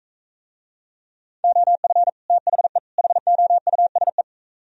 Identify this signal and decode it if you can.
Morse code, heard as a keyed tone, about 42 wpm, 700 Hz. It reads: OF THE HOUSE